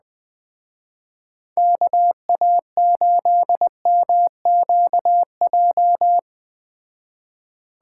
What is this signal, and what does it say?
Morse code, heard as a keyed tone, about 20 wpm, 700 Hz. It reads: KA8MQJ
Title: Speed Racing - USA 2x3 Call Signs 17wpm